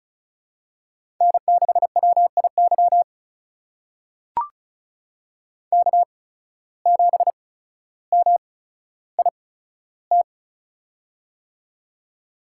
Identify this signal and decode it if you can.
Morse code, heard as a keyed tone, about 36 wpm, spaced as for 15 wpm, 700 Hz. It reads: N6WIY K7MIT
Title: USA Call Signs - 5 Characters - 8x Character Spacing - 35wpm